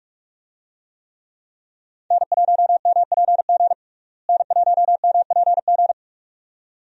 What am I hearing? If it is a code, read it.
Morse code, heard as a keyed tone, about 46 wpm, 700 Hz. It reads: N1MPG N1MPG
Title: USA Call Signs - 5 Characters - Repeated 2x - 45wpm